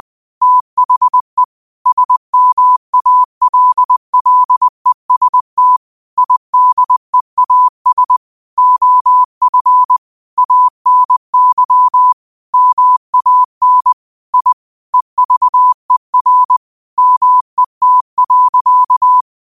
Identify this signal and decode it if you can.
Morse code, heard as a keyed tone, about 20 wpm, 1000 Hz. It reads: THE SMALLEST IDEAS OF ANY MAN I EVER MET.